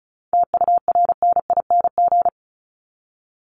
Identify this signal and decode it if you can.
Morse code, heard as a keyed tone, about 34 wpm, 700 Hz. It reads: TURNING
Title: ING Suffix - Top 100 Words with Base Words 35wpm